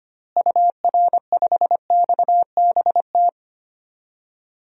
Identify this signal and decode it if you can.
Morse code, heard as a keyed tone, about 25 wpm, 700 Hz. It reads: UR5XBT
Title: DXCC-100 Call Signs - 6 Characters - Repeated 3x - 25wpm